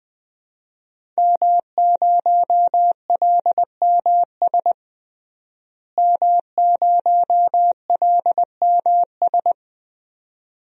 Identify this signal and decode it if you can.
Morse code, heard as a keyed tone, about 20 wpm, 700 Hz. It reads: M0LMS M0LMS
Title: DXCC-100 Call Signs - 5 Characters - Repeated 2x - 20wpm